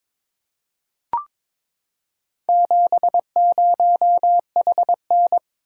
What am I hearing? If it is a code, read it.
Morse code, heard as a keyed tone, about 23 wpm, 700 Hz. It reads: 70HN
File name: practice / 22wpm / Four Letter-Number Combinations - 22wpm